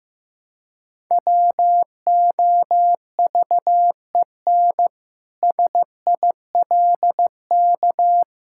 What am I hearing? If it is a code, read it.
Morse code, heard as a keyed tone, about 15 wpm, 700 Hz. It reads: WOVEN SILK
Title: Pangram Challenge v2 - 15wpm